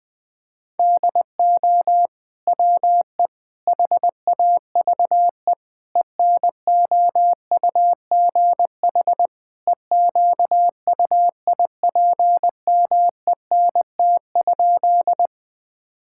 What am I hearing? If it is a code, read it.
Morse code, heard as a keyed tone, about 20 wpm, 700 Hz. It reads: DO WE HAVE ENOUGH EQUIPMENT?
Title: Sentences from Top 1000 Words - Review Part 5 - 20wpm